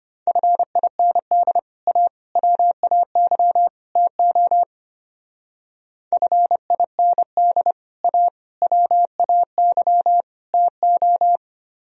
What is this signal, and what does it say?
Morse code, heard as a keyed tone, about 32 wpm, 700 Hz. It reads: FIND A WAY TO FIND A WAY TO